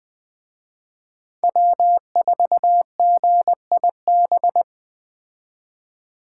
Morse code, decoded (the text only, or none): W4GIB